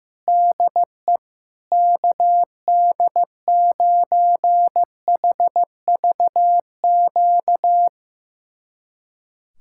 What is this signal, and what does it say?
Morse code, heard as a keyed tone, about 16 wpm, 700 Hz. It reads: DE KD9HVQ
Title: USA Callsign DE Callsign - 6 Characters - 15wpm